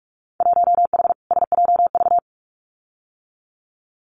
Morse code, heard as a keyed tone, about 46 wpm, 700 Hz. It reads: 1H SJV
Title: Winter Field Day Exchanges 45wpm